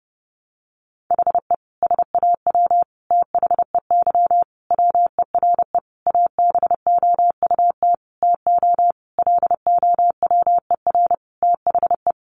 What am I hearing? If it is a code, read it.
Morse code, heard as a keyed tone, about 30 wpm, 700 Hz. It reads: HE SAW THEY WERE ABOUT TO LOWER THE
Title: Sentences from Top 700 Words 30wpm